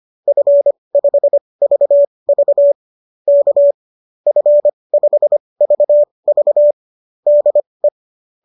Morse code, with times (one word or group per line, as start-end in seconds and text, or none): 0.3-2.7 F5VV
3.3-3.7 K
4.3-6.7 F5VV
7.3-7.9 DE